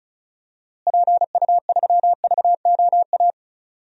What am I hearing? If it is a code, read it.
Morse code, heard as a keyed tone, about 36 wpm, 700 Hz. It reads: PU3VOA
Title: DXCC-100 Call Signs - 6 Characters - Repeated 3x - 35wpm